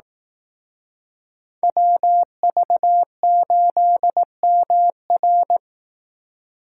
Morse code, heard as a keyed tone, about 18 wpm, 700 Hz. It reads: WV8MR
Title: Speed Racing - USA 2x2 Call Signs 15wpm